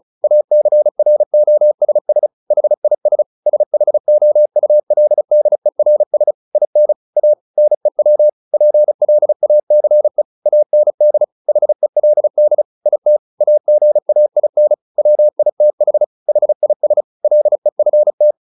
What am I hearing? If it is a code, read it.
Morse code, heard as a keyed tone, about 34 wpm, 600 Hz. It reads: ACROSS HIS SHOULDERS IN A NEW PLACE AND HELD IT AGAIN WITH HIS LEFT
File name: the old man and the sea 2 35wpm